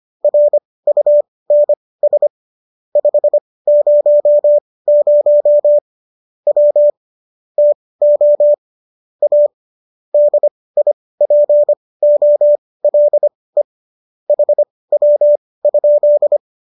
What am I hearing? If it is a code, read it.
Morse code, heard as a keyed tone, about 25 wpm, spaced as for 18 wpm, 590 Hz. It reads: RUNS 500 W TO A DIPOLE HW?